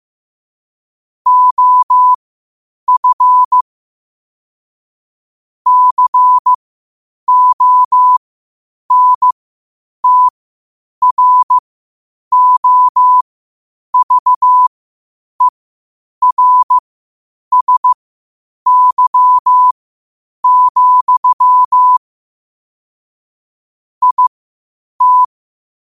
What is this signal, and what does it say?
Morse code, heard as a keyed tone, about 15 wpm, spaced as for 8 wpm, 1000 Hz. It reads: OF CONTROVERSY, IT
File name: qotdfiles / 8wpm